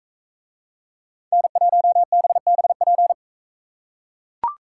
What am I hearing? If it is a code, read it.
Morse code, heard as a keyed tone, about 41 wpm, 700 Hz. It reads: N1BBP